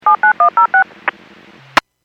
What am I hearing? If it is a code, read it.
DTMF, heard as telephone touch tones, tones of about 110 ms. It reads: 79106